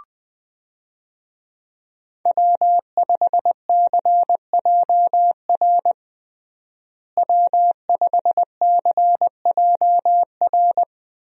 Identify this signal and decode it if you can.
Morse code, heard as a keyed tone, about 20 wpm, 700 Hz. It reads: W5CJR W5CJR
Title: USA Call Signs - 5 Characters - Repeated 3x - 20wpm